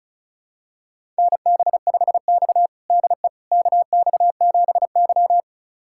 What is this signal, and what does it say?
Morse code, heard as a keyed tone, about 36 wpm, 700 Hz. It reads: NB5X DE KX7Y
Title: USA Callsign DE Callsign - 4 Characters - 35wpm